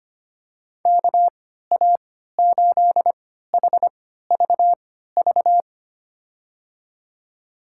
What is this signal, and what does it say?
Morse code, heard as a keyed tone, about 25 wpm, spaced as for 14 wpm, 700 Hz. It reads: KA8HVV
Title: USA Call Signs - 3x Character Spacing - 25wpm